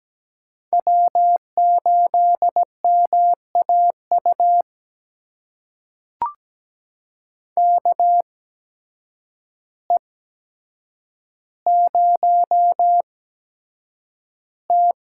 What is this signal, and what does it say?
Morse code, heard as a keyed tone, about 17 wpm, spaced as for 8 wpm, 700 Hz. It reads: W8MAU KE0T
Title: USA Call Signs - 5 Characters - 8x Character Spacing - 17wpm